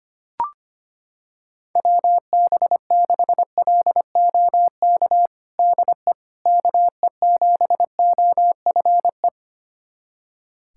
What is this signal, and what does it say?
Morse code, heard as a keyed tone, about 25 wpm, 700 Hz. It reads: WB6LOK DE KE7OFE
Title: USA Callsign DE Callsign - 6 Characters - 25wpm